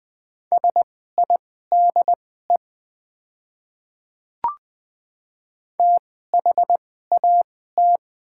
Morse code, heard as a keyed tone, about 20 wpm, spaced as for 14 wpm, 700 Hz. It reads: SIDE THAT
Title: 2x Character Spacing - Top 200 Words - 20wpm